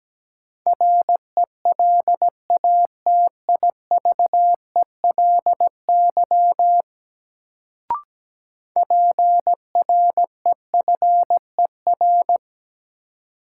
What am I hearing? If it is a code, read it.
Morse code, heard as a keyed tone, about 17 wpm, 700 Hz. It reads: RELATIVELY PREFER
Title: Sentences from Top 1100 Words - New Word Drill Alt - 17wpm